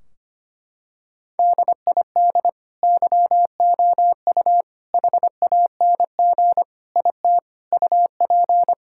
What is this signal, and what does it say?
Morse code, heard as a keyed tone, about 25 wpm, 700 Hz. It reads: DID YOU HANG IT UP
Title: Sentences from Top 1600 Words 25wpm